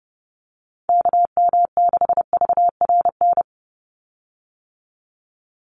Morse code, heard as a keyed tone, about 30 wpm, 700 Hz. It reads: KM6VRN
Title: USA Call Signs 30wpm